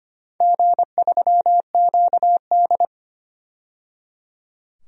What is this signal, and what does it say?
Morse code, heard as a keyed tone, about 25 wpm, 700 Hz. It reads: G3QD
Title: DXCC-100 Call Signs - 4 Characters - 25wpm